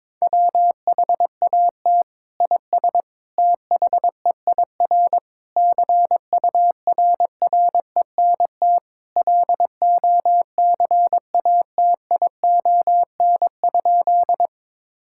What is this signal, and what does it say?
Morse code, heard as a keyed tone, about 22 wpm, 700 Hz. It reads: WHAT IS THEIR CURRENT LOCATION?
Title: Sentences from Top 1000 Words - Review Part 6 - 22wpm